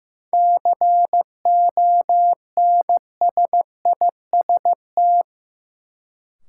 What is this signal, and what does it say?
Morse code, heard as a keyed tone, about 15 wpm, 700 Hz. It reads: CONSIST
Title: Sentences from Top 1400 Words - New Word Drill Alt - 15wpm